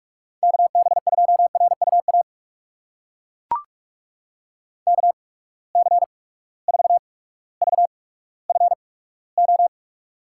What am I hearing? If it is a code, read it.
Morse code, heard as a keyed tone, about 42 wpm, spaced as for 19 wpm, 700 Hz. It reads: KB2RUA KC4VFO
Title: USA Call Signs - 8x Character Spacing - 45wpm